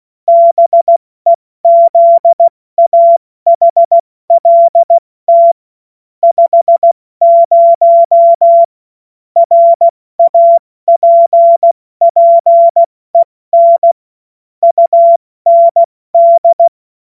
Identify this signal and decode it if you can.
Morse code, heard as a keyed tone, about 16 wpm, 680 Hz. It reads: BEZAHLT 50 RAPPEN UND
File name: Morsekurs / Witze / Witz 15. November 2022 70 F80